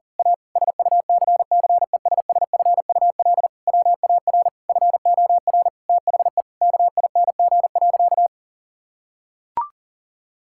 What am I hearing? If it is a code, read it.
Morse code, heard as a keyed tone, about 37 wpm, 700 Hz. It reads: A SUCCESSFUL WAR FOR THE KING.
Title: Sentences from Top 1000 Words - Review Part 6 - 40wpm